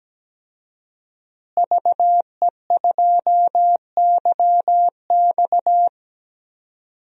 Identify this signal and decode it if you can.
Morse code, heard as a keyed tone, about 17 wpm, 700 Hz. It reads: VE2YX
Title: DXCC-100 Call Signs - 5 Characters - Repeated 2x - 17wpm